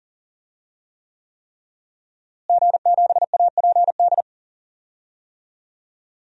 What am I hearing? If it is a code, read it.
Morse code, heard as a keyed tone, about 40 wpm, 700 Hz. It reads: G7APD
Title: DXCC-100 Call Signs - 5 Characters - Repeated 3x - 40wpm